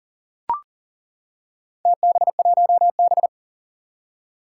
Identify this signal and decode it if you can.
Morse code, heard as a keyed tone, about 40 wpm, 700 Hz. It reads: TB1B
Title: Four Challenging Letter-Number Combinations 40wpm